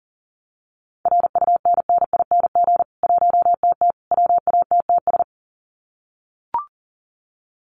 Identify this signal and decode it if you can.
Morse code, heard as a keyed tone, about 40 wpm, 700 Hz. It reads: RUNNING 1TT WATTS
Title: Common QSO Phrases 40wpm